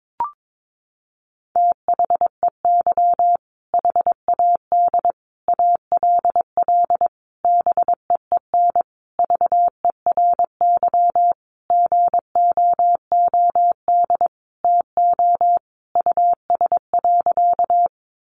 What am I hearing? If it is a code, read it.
Morse code, heard as a keyed tone, about 22 wpm, 700 Hz. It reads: THEY HAD ALL BEEN VERY GOOD TO US.
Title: Sentences from Top 200 Words 22wpm